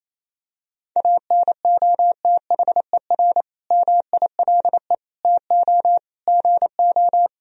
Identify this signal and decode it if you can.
Morse code, heard as a keyed tone, about 28 wpm, 700 Hz. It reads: ANOTHER MILE TO GO
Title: Sentences from Top 1000 Words - Review Part 1 - 28wpm